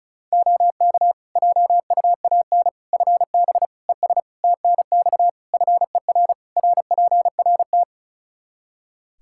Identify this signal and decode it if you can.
Morse code, heard as a keyed tone, about 36 wpm, 700 Hz. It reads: OK JUAN FB ES TNX FER RPRT
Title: K1USN SST QSO Practice 35wpm